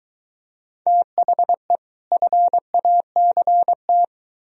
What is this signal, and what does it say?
Morse code, heard as a keyed tone, about 22 wpm, 700 Hz. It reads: THE FACT